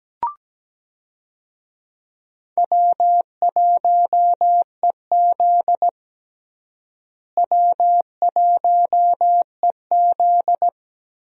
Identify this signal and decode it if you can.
Morse code, heard as a keyed tone, about 17 wpm, 700 Hz. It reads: W1EZ W1EZ
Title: USA Call Signs - 4 Characters - Repeated 2x - 17wpm